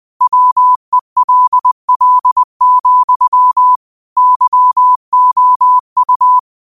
Morse code, heard as a keyed tone, about 20 wpm, 1000 Hz. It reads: WELL, YOU